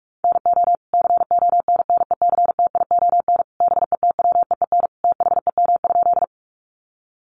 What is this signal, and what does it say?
Morse code, heard as a keyed tone, about 44 wpm, 700 Hz. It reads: NO CONNECTION BETWEEN THEM?
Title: Sentences from Top 1300 Words 45wpm